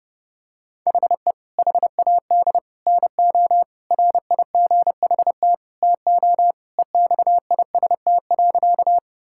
Morse code, heard as a keyed tone, about 32 wpm, 700 Hz. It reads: HE HAD NO RIGHT TO EXIST.
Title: Sentences from Top 1000 Words - Review Part 5 - 30wpm